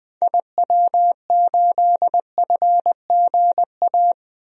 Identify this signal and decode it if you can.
Morse code, heard as a keyed tone, about 20 wpm, 700 Hz. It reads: IW8FGA
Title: DXCC-100 Call Signs - 6 Characters - Repeated 3x - 20wpm